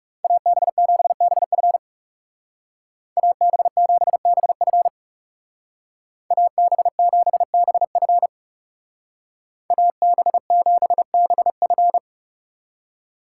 Morse code, heard as a keyed tone, about 37 wpm, 700 Hz. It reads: AB7BF AB7BF AB7BF AB7BF
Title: Speed Racing - USA 2x2 Call Signs 30wpm